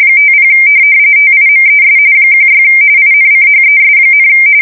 RTTY, about 45.45 baud. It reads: 50 TIMES 16 INCHES,
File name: fsk170 / USS-New-Jersey